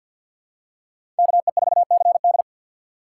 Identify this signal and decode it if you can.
Morse code, heard as a keyed tone, about 53 wpm, 700 Hz. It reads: KE4CD
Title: USA Call Signs - 5 Characters - 2x Character Spacing - 50wpm